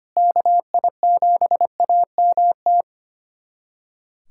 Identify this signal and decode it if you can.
Morse code, heard as a keyed tone, about 25 wpm, 700 Hz. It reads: KI7AMT
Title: USA Callsign DE Callsign - 6 Characters - 25wpm